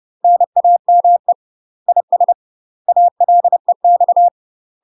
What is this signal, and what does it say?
Morse code, heard as a keyed tone, about 30 wpm, 700 Hz. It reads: NAME IS ALEX
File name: UK MASTER QSOs TRAINING / 30WPM / UK MASTER QSOs 0002